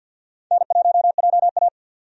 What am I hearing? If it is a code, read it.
Morse code, heard as a keyed tone, about 51 wpm, 700 Hz. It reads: N1JA
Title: USA Call Signs - 4 Characters - 50wpm